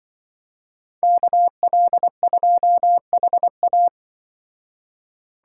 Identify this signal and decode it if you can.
Morse code, heard as a keyed tone, about 24 wpm, 700 Hz. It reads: KL2HA
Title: Speed Racing - USA 2x2 Call Signs 20wpm